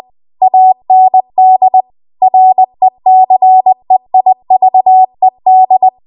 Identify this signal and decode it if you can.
Morse code, heard as a keyed tone, about 20 wpm, 750 Hz. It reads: AND RECEIVED
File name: Archive / 20 WPM / 20WPM